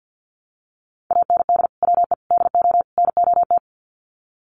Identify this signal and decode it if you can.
Morse code, heard as a keyed tone, about 50 wpm, 700 Hz. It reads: AND WE DO NOT
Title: Sets of 4 Words 50wpm